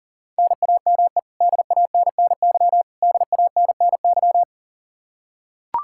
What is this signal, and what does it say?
Morse code, heard as a keyed tone, about 40 wpm, 700 Hz. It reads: NAME DANNY DANNY
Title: Practice QSO - Exchange 1 - Beginning - 40wpm